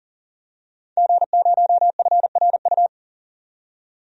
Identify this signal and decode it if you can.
Morse code, heard as a keyed tone, about 40 wpm, 700 Hz. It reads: G0FRU
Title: DXCC-100 Call Signs - 5 Characters - 40wpm